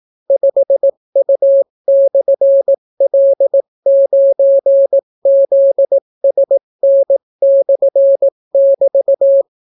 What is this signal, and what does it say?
Morse code, heard as a keyed tone, about 18 wpm, 550 Hz. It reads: 5U/L9ZSN/<BT>